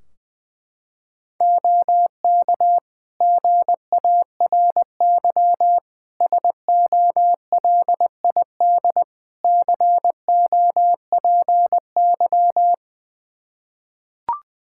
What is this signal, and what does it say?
Morse code, heard as a keyed tone, about 20 wpm, 700 Hz. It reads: OK GARY SOLID COPY
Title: Practice QSO - Exchange 3 - More Info - 20wpm